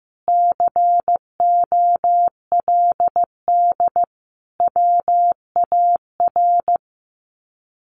Morse code, heard as a keyed tone, about 15 wpm, 700 Hz. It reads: COLD WAR
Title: Sets of 5 Words Encore 15wpm